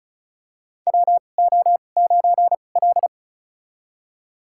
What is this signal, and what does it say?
Morse code, heard as a keyed tone, about 36 wpm, spaced as for 25 wpm, 700 Hz. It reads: WO9L
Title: USA Call Signs - 4 Characters - 2x Character Spacing - 35wpm